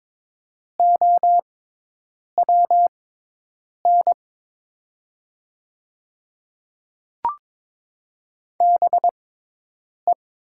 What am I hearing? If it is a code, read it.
Morse code, heard as a keyed tone, about 22 wpm, spaced as for 8 wpm, 700 Hz. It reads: OWN BE
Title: 6x Character Spacing - Top 100 Words - 22wpm